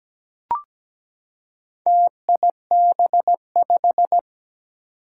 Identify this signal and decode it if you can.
Morse code, heard as a keyed tone, about 17 wpm, 700 Hz. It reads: TIB5
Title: Four Challenging Letter-Number Combinations 17wpm